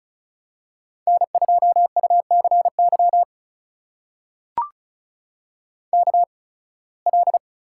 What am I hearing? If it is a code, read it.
Morse code, heard as a keyed tone, about 36 wpm, spaced as for 23 wpm, 700 Hz. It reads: N2UCY KL
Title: USA Call Signs - 5 Characters - 8x Character Spacing - 35wpm